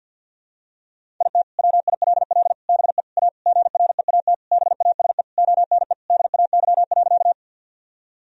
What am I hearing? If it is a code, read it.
Morse code, heard as a keyed tone, about 52 wpm, 700 Hz. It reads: IT WILL BE A GREAT BASE ONE DAY.
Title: Sentences from Top 1000 Words - Review Part 1 - 50wpm